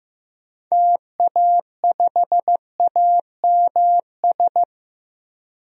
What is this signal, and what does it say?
Morse code, heard as a keyed tone, about 15 wpm, 700 Hz. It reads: TA5AMS
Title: DXCC-100 Call Signs - 6 Characters - Repeated 3x - 15wpm